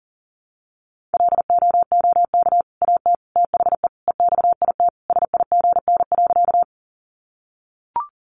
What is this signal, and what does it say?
Morse code, heard as a keyed tone, about 40 wpm, 700 Hz. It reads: LOOK AT THE EXIT SIGN.
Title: Sentences from Top 1100 Words 40wpm